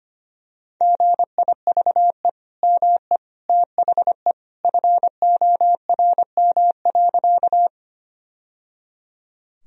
Morse code, heard as a keyed tone, about 25 wpm, 700 Hz. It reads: GIVE ME THE FORM.
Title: Sentences from Top 1000 Words - Review Part 1 - 25wpm